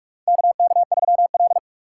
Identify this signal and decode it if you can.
Morse code, heard as a keyed tone, about 47 wpm, 700 Hz. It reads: KK3L
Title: USA Call Signs - 4 Characters - 3x Character Spacing - 45wpm